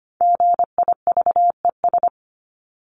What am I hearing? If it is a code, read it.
Morse code, heard as a keyed tone, about 25 wpm, 700 Hz. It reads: GIVES
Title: ES:S Suffix - Top 100 Words with Base Words 25wpm